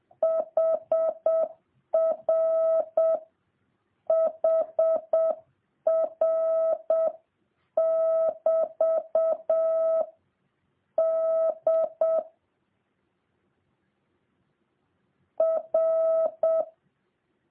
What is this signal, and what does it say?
Morse code, heard as a keyed tone, about 7 wpm, 650 Hz. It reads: HR HR <BT> D R